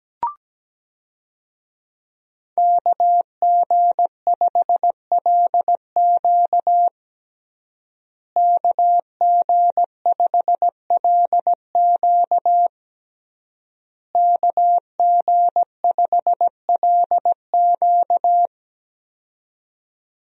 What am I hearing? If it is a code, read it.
Morse code, heard as a keyed tone, about 17 wpm, 700 Hz. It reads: KG5LQ KG5LQ KG5LQ
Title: USA Call Signs - 5 Characters - Repeated 3x - 17wpm